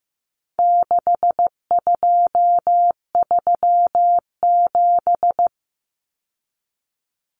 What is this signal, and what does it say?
Morse code, heard as a keyed tone, about 15 wpm, 700 Hz. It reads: 6237
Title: Numbers - 4-Digit 15wpm